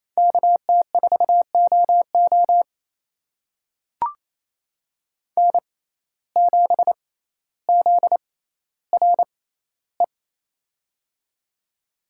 Morse code, heard as a keyed tone, about 28 wpm, spaced as for 14 wpm, 700 Hz. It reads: KT4OO N7ZRE
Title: USA Call Signs - 5 Characters - 6x Character Spacing - 28wpm